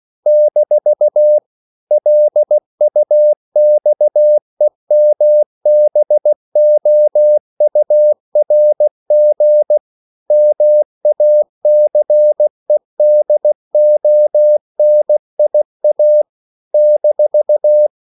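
Morse code, heard as a keyed tone, about 16 wpm, 600 Hz. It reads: - LUXEMBOURG MACEDONIA -
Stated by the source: CW practice of national capitals at 16 words per minute, from ON6ZQ.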